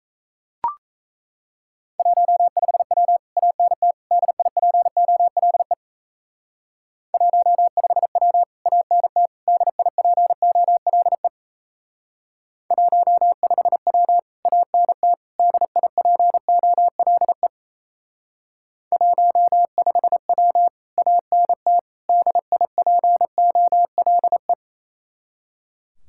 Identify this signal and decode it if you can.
Morse code, heard as a keyed tone, about 35 wpm, 700 Hz. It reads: 15W ANT DIPOLE 15W ANT DIPOLE 15W ANT DIPOLE 15W ANT DIPOLE
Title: Speed-Racing Common QSO Phrases - 28wpm